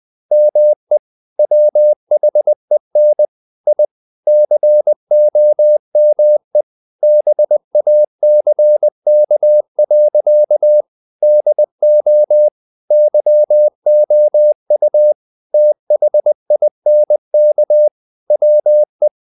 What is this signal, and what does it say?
Morse code, heard as a keyed tone, about 20 wpm, 600 Hz. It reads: ME WHEN I COME BACK. DO YOU THINK WE
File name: the old man and the sea / the old man and the sea part 1 20wpm